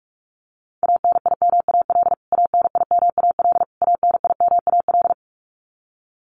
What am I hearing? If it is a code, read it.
Morse code, heard as a keyed tone, about 44 wpm, 700 Hz. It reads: ANIMAL ANIMAL ANIMAL
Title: Top 300 Words Repeated 45wpm